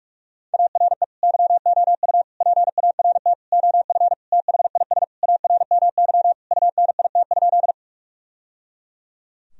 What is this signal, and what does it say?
Morse code, heard as a keyed tone, about 46 wpm, 700 Hz. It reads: ARE YOU PART OF THIS ARMY UNIT?
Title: Sentences from Top 1000 Words - Review Part 2 - 45wpm